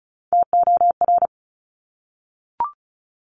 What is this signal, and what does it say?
Morse code, heard as a keyed tone, about 36 wpm, 700 Hz. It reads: TOR
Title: Top Three Letter Combinations in Words 35wpm